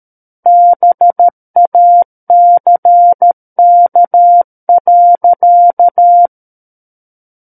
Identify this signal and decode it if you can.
Morse code, heard as a keyed tone, about 13 wpm, 700 Hz. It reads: BACK.
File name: Book One / 04-Chapter 4 - 13 WPM